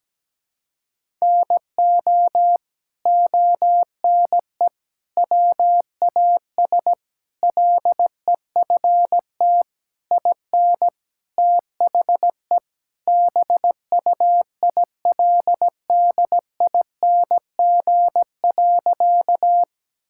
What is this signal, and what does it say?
Morse code, heard as a keyed tone, about 17 wpm, 700 Hz. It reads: NO ONE WAS LEFT IN THE BUILDING.
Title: Sentences from Top 1000 Words - Review Part 5 - 17wpm